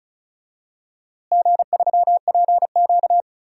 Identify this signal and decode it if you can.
Morse code, heard as a keyed tone, about 36 wpm, 690 Hz. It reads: G3PQ
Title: DXCC-100 Call Signs - 4 Characters - Repeated 3x - 35wpm